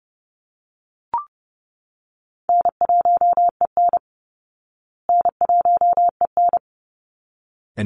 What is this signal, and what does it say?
Morse code, heard as a keyed tone, about 30 wpm, 700 Hz. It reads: N1EN N1EN
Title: Call Signs Repeated 3x - 30wpm